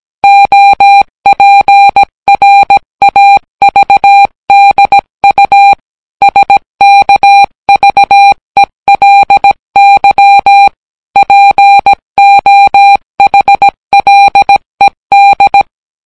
Morse code, heard as a keyed tone, about 17 wpm, 800 Hz. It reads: OPRAVDU SKVELY POHLED